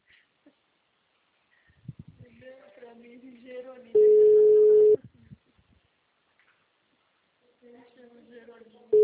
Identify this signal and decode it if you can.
Ringback tone (Europe, 425 Hz), with speech.
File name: out- -2006- - - .157